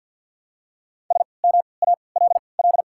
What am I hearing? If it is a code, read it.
Morse code, heard as a keyed tone, about 54 wpm, spaced as for 29 wpm, 700 Hz. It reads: SMALL
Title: 3x Character Spacing - Top 500 Words - 50wpm